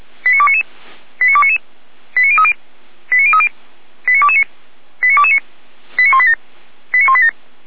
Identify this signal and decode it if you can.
Five-tone selective call (ZVEI), heard as quick runs of five tones, tones of about 80 ms. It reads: 89299 89299 89030 89030 89229 89229 89187 89187